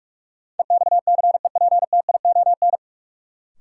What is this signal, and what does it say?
Morse code, heard as a keyed tone, about 46 wpm, 700 Hz. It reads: EXCEPTION
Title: Sets of 4 Words - New Word Drill - 45wpm